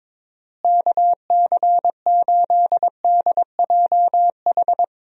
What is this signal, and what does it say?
Morse code, heard as a keyed tone, about 22 wpm, 700 Hz. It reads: KC8DJH
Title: USA Call Signs - 6 Characters - Repeated 3x - 22wpm